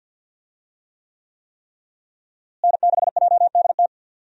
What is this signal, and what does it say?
Morse code, heard as a keyed tone, about 52 wpm, 700 Hz. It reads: N6JDT
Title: USA Call Signs - 5 Characters - Repeated 3x - 50wpm